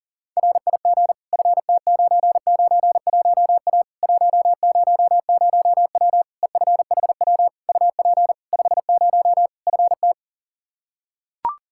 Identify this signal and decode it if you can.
Morse code, heard as a keyed tone, about 40 wpm, 700 Hz. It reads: RIG FT991A 100W EFHW UP 50 FT
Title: Practice QSO - Exchange 2 - Info - 40wpm